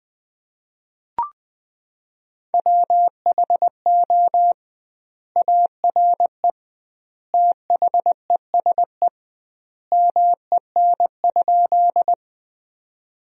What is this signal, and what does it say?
Morse code, heard as a keyed tone, about 21 wpm, 700 Hz. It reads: WHO ARE THESE MEN?
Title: 2x Word-Spacing - Sentences from Top 200 Words - 20wpm